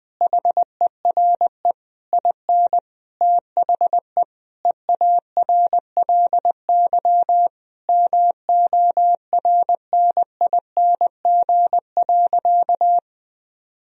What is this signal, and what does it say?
Morse code, heard as a keyed tone, about 20 wpm, 700 Hz. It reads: HERE IN THE EARLY MORNING.
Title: Sentences from Top 1000 Words - Review Part 3 - 20wpm